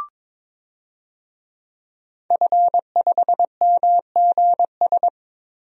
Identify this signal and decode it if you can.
Morse code, heard as a keyed tone, about 22 wpm, 700 Hz. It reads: F5MGS
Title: DXCC-100 Call Signs - 5 Characters - Repeated 3x - 22wpm